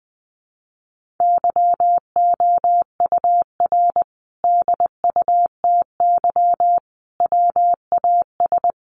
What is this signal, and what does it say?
Morse code, heard as a keyed tone, about 20 wpm, 700 Hz. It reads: YOUR DUTY WAS
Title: Sentences from Top 1500 Words 20wpm